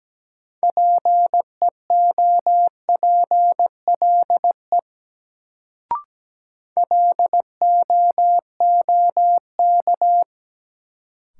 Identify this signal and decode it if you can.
Morse code, heard as a keyed tone, about 17 wpm, 700 Hz. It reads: PEOPLE LOOK
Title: Sets of 2 Words - New Word Drill - 17wpm